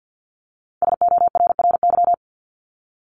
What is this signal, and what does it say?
Morse code, heard as a keyed tone, about 50 wpm, 700 Hz. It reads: SORRY
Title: Top 1751-2000 Words 50wpm